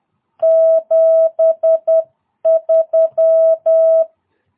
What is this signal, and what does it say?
Morse code, heard as a keyed tone, about 10 wpm, 650 Hz. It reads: 73
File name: jh8cbh / 2022cw